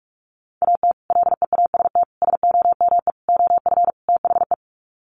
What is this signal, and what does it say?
Morse code, heard as a keyed tone, about 44 wpm, 700 Hz. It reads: AT LEAST SOME OF THE
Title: Sets of 5 Words Encore 45wpm